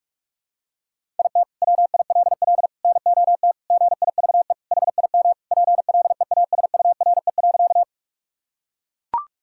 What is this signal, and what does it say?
Morse code, heard as a keyed tone, about 41 wpm, 700 Hz. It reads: IT WILL NOT GIVE HIM PLEASURE.
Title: Sentences from Top 1600 Words 45wpm